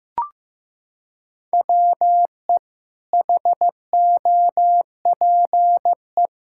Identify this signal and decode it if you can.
Morse code, heard as a keyed tone, about 16 wpm, 700 Hz. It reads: WE HOPE
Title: Sentences from Top 1000 Words - Review Part 6 - 15wpm